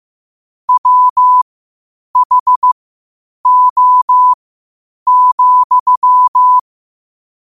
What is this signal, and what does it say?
Morse code, heard as a keyed tone, about 15 wpm, spaced as for 8 wpm, 1000 Hz. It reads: WHO,
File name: qotdfiles / 8wpm